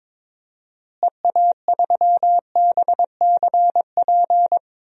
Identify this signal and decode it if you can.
Morse code, heard as a keyed tone, about 21 wpm, 700 Hz. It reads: EA3BCP
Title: DXCC-100 Call Signs - 6 Characters - Repeated 3x - 22wpm